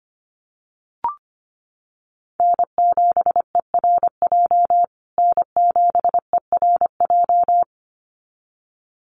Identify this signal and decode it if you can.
Morse code, heard as a keyed tone, about 25 wpm, 700 Hz. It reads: N7ERJ N7ERJ
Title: USA Call Signs Repeated 2x 25wpm